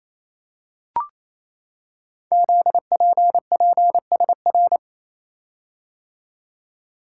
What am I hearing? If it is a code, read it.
Morse code, heard as a keyed tone, about 28 wpm, 700 Hz. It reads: ZPPSR
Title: Code Groups - 5 Letters - 28wpm